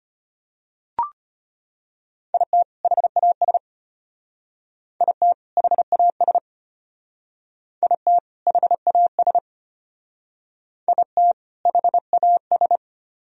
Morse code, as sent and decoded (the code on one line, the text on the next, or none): .. - / .... .- ... / .. - / .... .- ... / .. - / .... .- ... / .. - / .... .- ...
IT HAS IT HAS IT HAS IT HAS